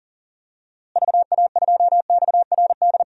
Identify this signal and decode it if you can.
Morse code, heard as a keyed tone, about 40 wpm, 700 Hz. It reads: VA2XRD
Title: DXCC-100 Call Signs - 6 Characters - Repeated 3x - 40wpm